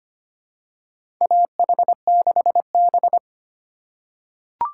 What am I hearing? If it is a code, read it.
Morse code, heard as a keyed tone, about 25 wpm, 700 Hz. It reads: AH6B
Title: USA Call Signs - 4 Characters - 2x Character Spacing - 25wpm